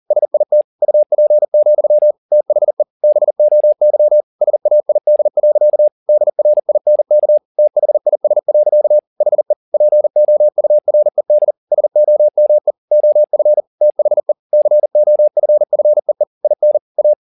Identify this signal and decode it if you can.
Morse code, heard as a keyed tone, about 40 wpm, 600 Hz. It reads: SIT UP, THE BOY SAID. DRINK THIS. HE POURED SOME OF THE COFFEE IN A
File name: the old man and the sea 2 40wpm